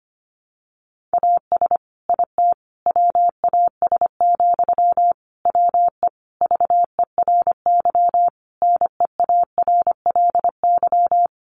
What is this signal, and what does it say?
Morse code, heard as a keyed tone, about 25 wpm, 700 Hz. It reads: AS IT WAS, WE VERY NEARLY
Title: Sentences from Top 1400 Words 25wpm